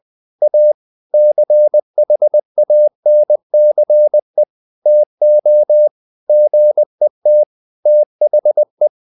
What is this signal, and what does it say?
Morse code, heard as a keyed tone, about 20 wpm, 600 Hz. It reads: A CHANCE TO GET THE
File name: the old man and the sea part 2 20wpm